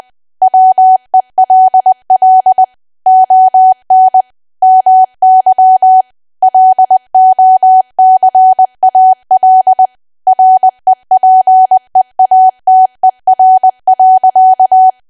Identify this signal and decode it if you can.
Morse code, heard as a keyed tone, about 20 wpm, 750 Hz. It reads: WELL ON MY LOCAL REPEATER.
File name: Archive / 20 WPM / 20WPM